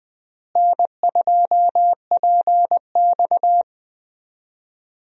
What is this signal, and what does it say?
Morse code, heard as a keyed tone, about 20 wpm, 700 Hz. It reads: N2PX
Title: USA Call Signs - 4 Characters - Repeated 3x - 20wpm